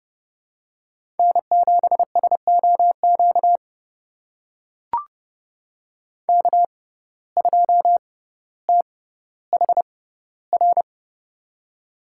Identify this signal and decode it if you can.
Morse code, heard as a keyed tone, about 30 wpm, spaced as for 15 wpm, 690 Hz. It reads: N7SOQ K2THR
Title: USA Call Signs - 5 Characters - 6x Character Spacing - 30wpm